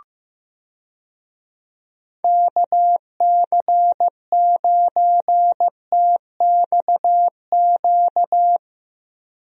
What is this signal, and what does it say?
Morse code, heard as a keyed tone, about 15 wpm, 700 Hz. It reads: KC9TXQ
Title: USA Call Signs - 6 Characters - Repeated 2x - 15wpm